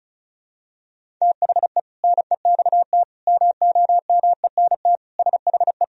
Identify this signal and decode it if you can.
Morse code, heard as a keyed tone, about 36 wpm, 700 Hz. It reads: THE NEXT MOMENT SHE
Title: Sentences from Top 1000 Words - Review Part 1 - 35wpm